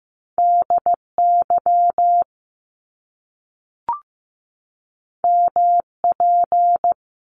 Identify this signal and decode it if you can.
Morse code, heard as a keyed tone, about 15 wpm, 700 Hz. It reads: DY MP
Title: Top Two Letter Combinations in Words 15wpm